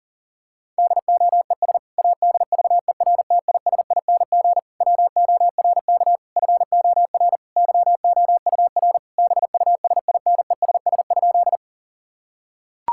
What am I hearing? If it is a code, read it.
Morse code, heard as a keyed tone, about 40 wpm, 700 Hz. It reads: DOES ADVERTISING WORK FOR YOUR BUSINESS?
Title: Sentences from Top 1000 Words - Review Part 6 - 40wpm